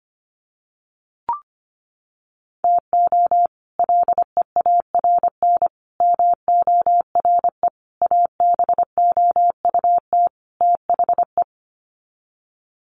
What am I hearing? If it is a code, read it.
Morse code, heard as a keyed tone, about 25 wpm, 700 Hz. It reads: TO LEARN MORE ABOUT THE
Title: Sets of 5 Words Encore2 25wpm